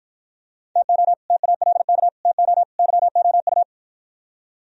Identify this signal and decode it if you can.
Morse code, heard as a keyed tone, about 53 wpm, 700 Hz. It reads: TO TALK TO YOU